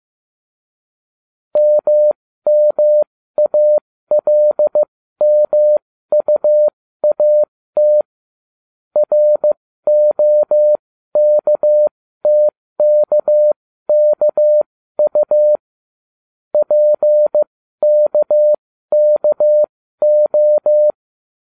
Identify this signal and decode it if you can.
Morse code, heard as a keyed tone, about 15 wpm, spaced as for 13 wpm, 600 Hz. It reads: MMALMUAT ROKTKKU PKKO